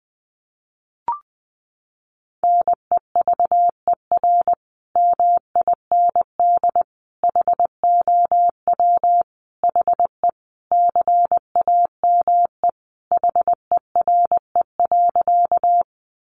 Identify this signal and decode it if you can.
Morse code, heard as a keyed tone, about 19 wpm, 700 Hz. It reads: NEVER MIND HOW HE CAME HERE.
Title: Sentences from Top 500 Words 20wpm